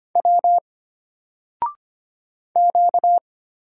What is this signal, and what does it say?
Morse code, heard as a keyed tone, about 25 wpm, 700 Hz. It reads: W Q
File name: Single-Letter-Number-25wpm